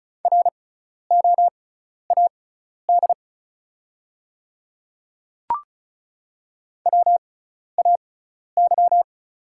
Morse code, heard as a keyed tone, about 36 wpm, spaced as for 12 wpm, 700 Hz. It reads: ROAD WAY
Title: 6x Character Spacing - Top 500 Words - 35wpm